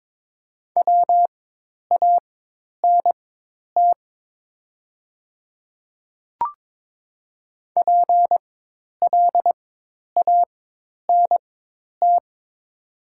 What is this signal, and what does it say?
Morse code, heard as a keyed tone, about 23 wpm, spaced as for 10 wpm, 700 Hz. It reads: WANT PLANT
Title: 4x Character Spacing - Top 300 Words - 22wpm